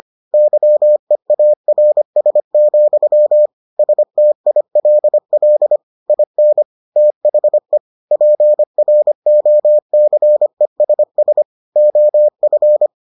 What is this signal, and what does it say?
Morse code, heard as a keyed tone, about 25 wpm, 600 Hz. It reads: YEARS, STILL IN THE PROCESS OF